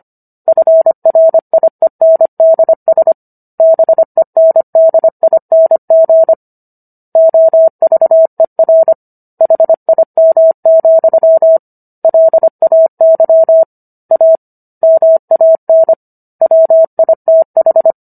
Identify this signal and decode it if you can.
Morse code, heard as a keyed tone, about 25 wpm, 650 Hz. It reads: FRIENDS BENDING OVER HIM, LAY A MAN WITH